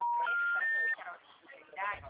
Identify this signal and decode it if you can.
Special information tone, with speech.